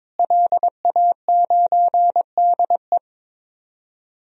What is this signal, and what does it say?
Morse code, heard as a keyed tone, about 22 wpm, 700 Hz. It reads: LA9DE
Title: DXCC-100 Call Signs - 5 Characters - Repeated 2x - 22wpm